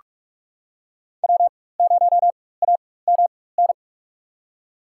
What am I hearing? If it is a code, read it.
Morse code, heard as a keyed tone, about 44 wpm, spaced as for 21 wpm, 700 Hz. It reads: W0AMN
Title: USA Call Signs - 4x Character Spacing - 45wpm